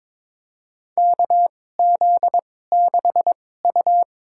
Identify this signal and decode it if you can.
Morse code, heard as a keyed tone, about 22 wpm, spaced as for 16 wpm, 700 Hz. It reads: KZ6U
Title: USA Call Signs - 4 Characters - 2x Character Spacing - 22wpm